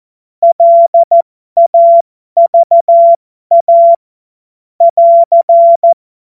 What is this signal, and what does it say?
Morse code, heard as a keyed tone, about 14 wpm, spaced as for 12 wpm, 680 Hz. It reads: LAVA <AR>
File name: Morsekurs / Witze